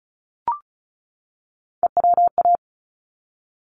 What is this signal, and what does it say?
Morse code, heard as a keyed tone, about 34 wpm, 700 Hz. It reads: EWA
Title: ARRL Sections 35wpm